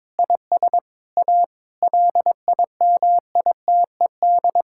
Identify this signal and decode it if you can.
Morse code, heard as a keyed tone, about 22 wpm, 700 Hz. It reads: IS A LIMITED
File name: Difficult Sentences with Hints - 22wpm